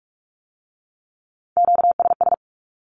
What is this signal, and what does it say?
Morse code, heard as a keyed tone, about 46 wpm, 700 Hz. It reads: QSS
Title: Q-Codes with Spoken Meaning - 45wpm